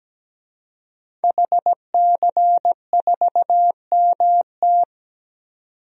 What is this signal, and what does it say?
Morse code, heard as a keyed tone, about 17 wpm, 700 Hz. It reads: HC4MT
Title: DXCC-100 Call Signs - 5 Characters - 17wpm